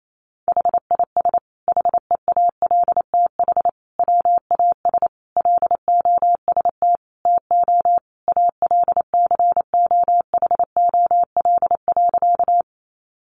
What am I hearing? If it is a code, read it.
Morse code, heard as a keyed tone, about 28 wpm, 700 Hz. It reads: HIS HEALTH WAS LOST TO ALCOHOL.
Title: Sentences from Top 1400 Words 28wpm